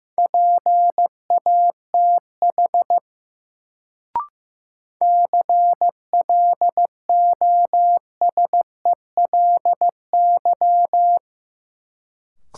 Morse code, heard as a keyed tone, about 15 wpm, 700 Hz. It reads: PATH CLOSELY
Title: Sentences from Top 1400 Words - New Word Drill Alt - 15wpm